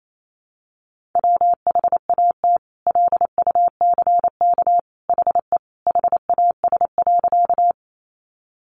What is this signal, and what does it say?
Morse code, heard as a keyed tone, about 28 wpm, 700 Hz. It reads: WHAT LUCK HE HAS.
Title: Sentences from Top 1500 Words 28wpm